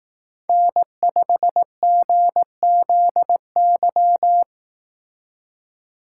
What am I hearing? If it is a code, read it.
Morse code, heard as a keyed tone, about 18 wpm, 700 Hz. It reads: N5GZY
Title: Speed Racing - USA 1x3 Call Signs 15wpm